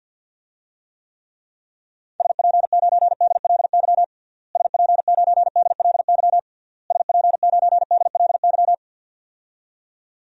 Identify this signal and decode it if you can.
Morse code, heard as a keyed tone, about 50 wpm, 700 Hz. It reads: SP9DLY SP9DLY SP9DLY
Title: DXCC-100 Call Signs - 6 Characters - Repeated 3x - 50wpm